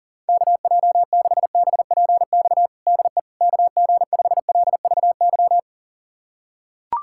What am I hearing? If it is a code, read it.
Morse code, heard as a keyed tone, about 37 wpm, 700 Hz. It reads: KJ6BPX DE KG5LVY
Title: USA Callsign DE Callsign - 6 Characters - 40wpm